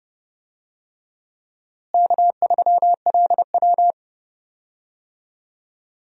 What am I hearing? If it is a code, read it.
Morse code, heard as a keyed tone, about 30 wpm, 700 Hz. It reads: K3LW
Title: USA Call Signs - 4 Characters - Repeated 3x - 30wpm